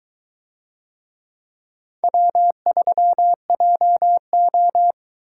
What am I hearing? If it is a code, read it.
Morse code, heard as a keyed tone, about 22 wpm, 700 Hz. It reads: W3JO